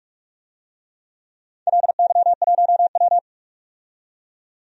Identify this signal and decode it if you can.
Morse code, heard as a keyed tone, about 46 wpm, 700 Hz. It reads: LY1W
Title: DXCC-100 Call Signs - 4 Characters - 45wpm